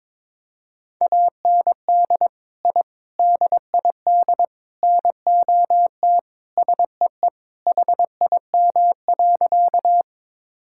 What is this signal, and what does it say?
Morse code, heard as a keyed tone, about 22 wpm, 700 Hz. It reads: AND I DID NOT SEE HIM.
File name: practice / 22wpm / Sentences with 4-Word Phrases - 22wpm